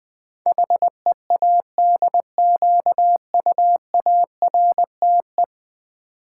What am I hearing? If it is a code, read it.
Morse code, heard as a keyed tone, about 20 wpm, 700 Hz. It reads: HEADQUARTE
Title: Build A Word - 5 Letters At A Time - 12-Letter Words - 20wpm